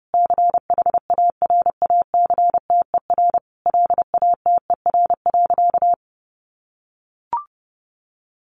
Morse code, heard as a keyed tone, about 28 wpm, 700 Hz. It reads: CHARACTER LATER.
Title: Sentences from Top 1200 Words 30wpm